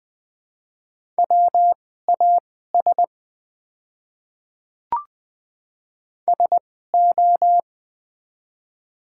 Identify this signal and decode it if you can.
Morse code, heard as a keyed tone, about 20 wpm, spaced as for 15 wpm, 700 Hz. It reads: WAS SO
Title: 2x Character Spacing - Top 100 Words - 20wpm